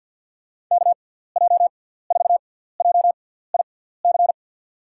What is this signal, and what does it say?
Morse code, heard as a keyed tone, about 47 wpm, spaced as for 17 wpm, 700 Hz. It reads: KJ4JIC
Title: USA Call Signs - 6 Characters - 6x Character Spacing - 50wpm